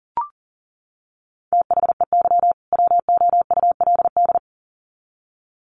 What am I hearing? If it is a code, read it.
Morse code, heard as a keyed tone, about 40 wpm, 700 Hz. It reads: THEY WOULD
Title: Sets of 2 Words 40wpm